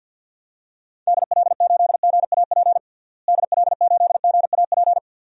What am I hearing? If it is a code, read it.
Morse code, heard as a keyed tone, about 45 wpm, 700 Hz. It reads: DL8GAP DL8GAP
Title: DXCC-100 Call Signs - 6 Characters - Repeated 2x - 50wpm